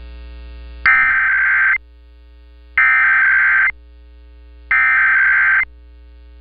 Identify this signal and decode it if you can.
Emergency Alert System (SAME) tones (header bursts), heard as harsh buzzing bursts. It is heard from 0.9 to 5.6 s.